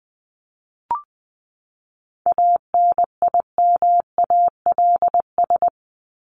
Morse code, heard as a keyed tone, about 20 wpm, 700 Hz. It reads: ANIMALS
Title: ES:S Suffix - Top 100 Words with Base Words 20wpm